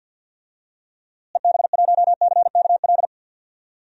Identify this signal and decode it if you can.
Morse code, heard as a keyed tone, about 46 wpm, 700 Hz. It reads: EB1CKL
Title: DXCC-100 Call Signs - 6 Characters - Repeated 2x - 50wpm